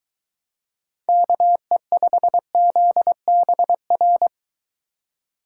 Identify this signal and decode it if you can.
Morse code, heard as a keyed tone, about 23 wpm, 700 Hz. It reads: KE5ZBR